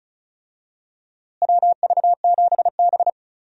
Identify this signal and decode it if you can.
Morse code, heard as a keyed tone, about 36 wpm, 700 Hz. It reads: WV7B
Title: USA Call Signs - 4 Characters - Repeated 2x - 35wpm